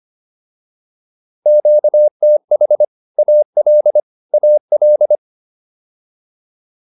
Morse code, heard as a keyed tone, about 25 wpm, 600 Hz. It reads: QTH AL AL
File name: QSO201 25